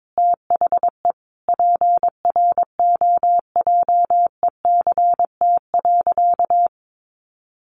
Morse code, heard as a keyed tone, about 22 wpm, 700 Hz. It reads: THE PROJECT.
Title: Sentences from Top 1400 Words 22wpm